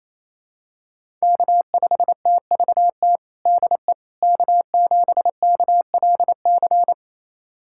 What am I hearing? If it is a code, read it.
Morse code, heard as a keyed tone, about 27 wpm, 700 Hz. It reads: K5TVT DE K7KLC
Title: USA Callsign DE Callsign - 5 Characters - 28wpm